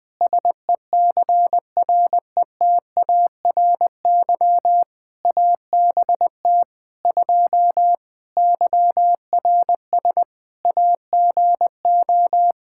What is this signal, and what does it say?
Morse code, heard as a keyed tone, about 21 wpm, 700 Hz. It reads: SECRETARY ABT 2 YRS AGO
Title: Practice QSO - Exchange 3 - More Info - 20wpm